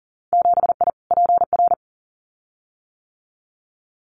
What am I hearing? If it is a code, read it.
Morse code, heard as a keyed tone, about 40 wpm, 700 Hz. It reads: 7I PR
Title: Winter Field Day Exchanges Encore 40wpm